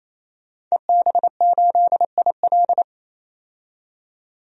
Morse code, heard as a keyed tone, about 28 wpm, 700 Hz. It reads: EB8IL
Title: DXCC-100 Call Signs - 5 Characters - 28wpm